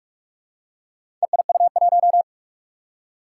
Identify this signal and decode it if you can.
Morse code, heard as a keyed tone, about 44 wpm, 700 Hz. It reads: EIU1
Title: Four Challenging Letter-Number Combinations 45wpm